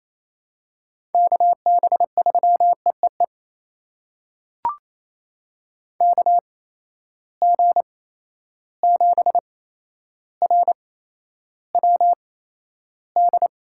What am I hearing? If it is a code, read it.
Morse code, heard as a keyed tone, about 28 wpm, spaced as for 12 wpm, 700 Hz. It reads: KB3EEE KG7RWD
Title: USA Call Signs - 6 Characters - 8x Character Spacing - 28wpm